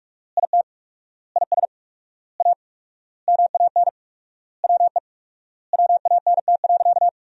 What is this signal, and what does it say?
Morse code, heard as a keyed tone, about 46 wpm, 700 Hz. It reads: IT IS A MAN WE WANT.
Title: 4x Word-Spacing - Sentences from Top 200 Words - 45wpm